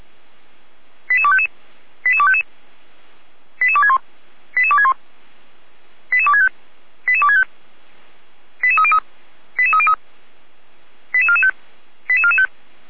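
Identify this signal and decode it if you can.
Five-tone selective call (ZVEI), heard as quick runs of five tones, tones of about 80 ms. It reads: 88255 88255 88261 88261 88265 88265 88303 88303 88445 88445
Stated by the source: Five-tone ZVEI selective call received on an SDR.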